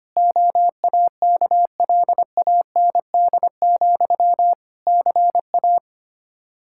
Morse code, heard as a keyed tone, about 25 wpm, 700 Hz. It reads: OAKLAND, CA
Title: K1USN SST QSO Practice 25wpm